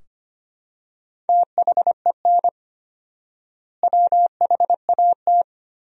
Morse code, heard as a keyed tone, about 25 wpm, 700 Hz. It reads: THEN WHAT
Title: 4x Word-Spacing - Sentences from Top 400 Words - 25wpm